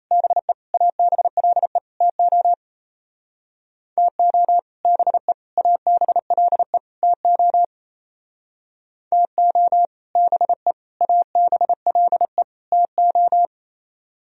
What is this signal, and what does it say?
Morse code, heard as a keyed tone, about 34 wpm, 700 Hz. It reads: BE ABLE TO TO BE ABLE TO TO BE ABLE TO
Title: Speed-Racing - Sets of 4 Words 28wpm